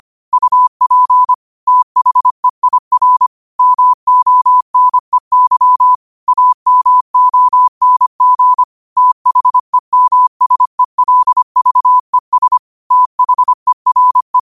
Morse code, heard as a keyed tone, about 25 wpm, 1000 Hz. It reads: UP THEIR MONEY AMONG THEMSELVES THERE